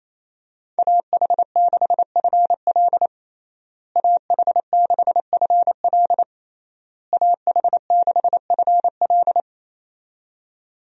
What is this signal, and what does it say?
Morse code, heard as a keyed tone, about 28 wpm, 700 Hz. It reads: AH6FL AH6FL AH6FL
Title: USA Call Signs - 5 Characters - Repeated 3x - 28wpm